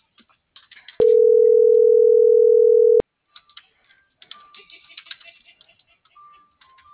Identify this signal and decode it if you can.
Ringback tone (North America), 440 + 480 Hz.